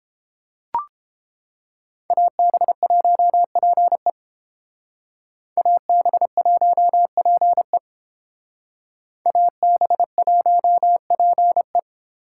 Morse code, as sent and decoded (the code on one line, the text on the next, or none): .- -... .---- .--. / .- -... .---- .--. / .- -... .---- .--. .
AB1PE AB1PE AB1PE